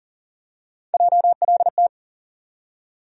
Morse code, heard as a keyed tone, about 40 wpm, 700 Hz. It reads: JLT
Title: USA Call Sign Postfixes - 3 Letters 40wpm